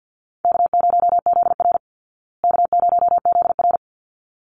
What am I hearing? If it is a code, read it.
Morse code, heard as a keyed tone, about 48 wpm, 700 Hz. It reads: K0ZR K0ZR
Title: Easy Call Signs Repeated 2x - 50wpm